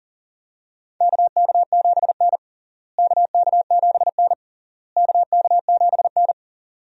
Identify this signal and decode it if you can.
Morse code, heard as a keyed tone, about 40 wpm, 700 Hz. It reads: KK7N KK7N KK7N
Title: USA Call Signs - 4 Characters - Repeated 3x - 40wpm